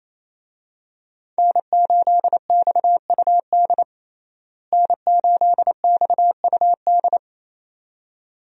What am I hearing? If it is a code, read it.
Morse code, heard as a keyed tone, about 28 wpm, 700 Hz. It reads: N8XUD N8XUD
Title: USA Call Signs - 5 Characters - Repeated 2x - 28wpm